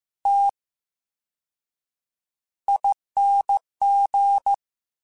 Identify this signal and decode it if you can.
Morse code, heard as a keyed tone, about 15 wpm, spaced as for 7 wpm, 800 Hz. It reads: TING
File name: Special INGs / Special INGs-mp3s / Slow ing @13